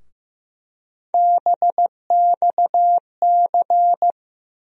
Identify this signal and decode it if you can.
Morse code, heard as a keyed tone, about 15 wpm, 700 Hz. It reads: BXC
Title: USA Call Sign Postfixes - 3 Letters 15wpm